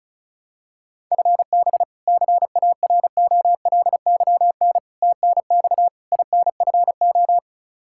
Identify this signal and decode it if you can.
Morse code, heard as a keyed tone, about 36 wpm, 700 Hz. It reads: FB CAROLYN TNX INFO
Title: Practice QSO - Exchange 3 - More Info - 35wpm